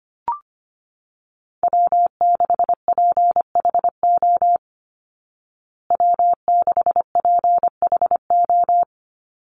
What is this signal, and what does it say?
Morse code, heard as a keyed tone, about 25 wpm, 700 Hz. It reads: W6PHO W6PHO
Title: Call Signs Repeated 2x - 25wpm